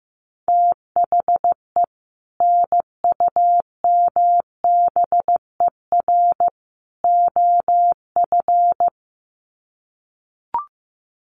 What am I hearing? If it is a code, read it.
Morse code, heard as a keyed tone, about 15 wpm, 700 Hz. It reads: THE NUMBER OF
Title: Sets of 3 Words 15wpm